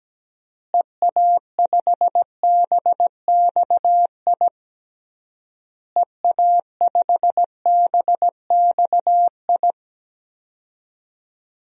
Morse code, heard as a keyed tone, about 17 wpm, 700 Hz. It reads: EA5BXI EA5BXI
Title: DXCC-100 Call Signs - 6 Characters - Repeated 3x - 17wpm